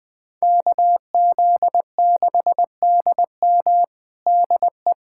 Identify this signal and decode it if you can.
Morse code, heard as a keyed tone, about 20 wpm, 700 Hz. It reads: KZ6DM DE
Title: USA Callsign DE Callsign - 5 Characters - 20wpm